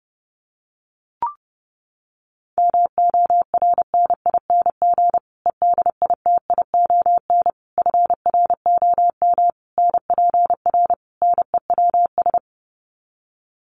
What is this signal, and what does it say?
Morse code, heard as a keyed tone, about 30 wpm, 700 Hz. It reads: MORNING EDITION FROM NPR NEWS
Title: Sets of 5 Words Encore 30wpm